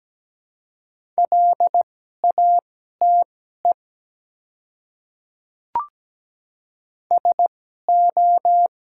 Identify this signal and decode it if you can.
Morse code, heard as a keyed tone, about 17 wpm, spaced as for 12 wpm, 700 Hz. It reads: LATE SO
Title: 2x Character Spacing - Top 300 Words - 17wpm